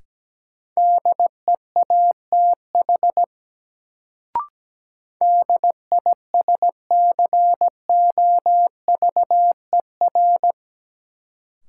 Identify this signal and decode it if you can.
Morse code, heard as a keyed tone, about 17 wpm, 700 Hz. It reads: DEATH DISCOVER
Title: Sentences from Top 1100 Words - New Word Drill Alt - 17wpm